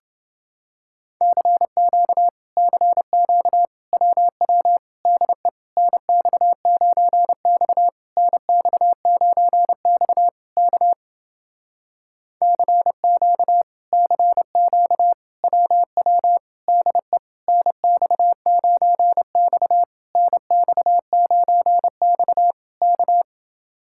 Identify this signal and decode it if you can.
Morse code, heard as a keyed tone, about 29 wpm, 700 Hz. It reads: CQ CQ WW DE NX9X NX9X K CQ CQ WW DE NX9X NX9X K